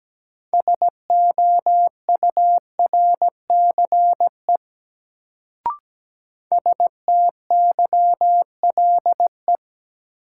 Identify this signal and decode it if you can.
Morse code, heard as a keyed tone, about 17 wpm, 700 Hz. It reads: SOURCE STYLE
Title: Sentences from Top 800 Words - New Word Drill - 17wpm